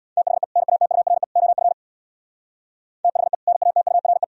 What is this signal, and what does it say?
Morse code, heard as a keyed tone, about 77 wpm, 700 Hz. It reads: THE NATURE OF THE NATURE